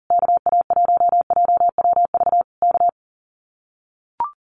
Morse code, heard as a keyed tone, about 40 wpm, 700 Hz. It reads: KA1JWV K
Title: Common QSO Phrases Encore 40wpm